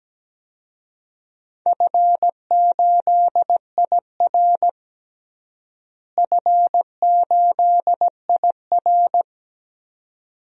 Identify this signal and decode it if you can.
Morse code, heard as a keyed tone, about 17 wpm, 700 Hz. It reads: F8IR F8IR
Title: DXCC-100 Call Signs - 4 Characters - Repeated 2x - 17wpm